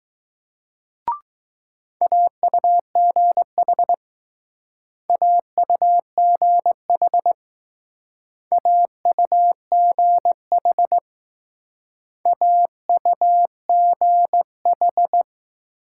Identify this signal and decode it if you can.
Morse code, heard as a keyed tone, about 20 wpm, 700 Hz. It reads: AUGH AUGH AUGH AUGH